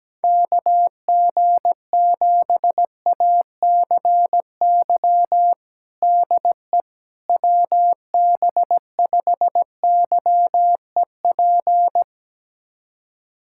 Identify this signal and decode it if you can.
Morse code, heard as a keyed tone, about 17 wpm, 700 Hz. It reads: KG7ACY DE WB5YEP
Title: USA Callsign DE Callsign - 6 Characters - 17wpm